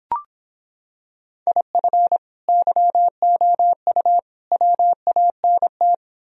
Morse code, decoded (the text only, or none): IF YOU WANT